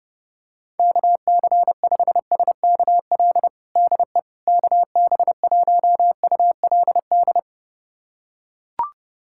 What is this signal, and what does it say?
Morse code, heard as a keyed tone, about 30 wpm, 700 Hz. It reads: KC5SKL DE KB1ULD
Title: USA Callsign DE Callsign - 6 Characters - 30wpm